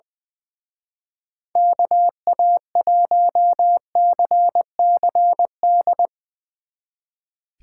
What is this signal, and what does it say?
Morse code, heard as a keyed tone, about 24 wpm, 700 Hz. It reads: KA1CCD
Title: Speed Racing - USA 2x3 Call Signs 20wpm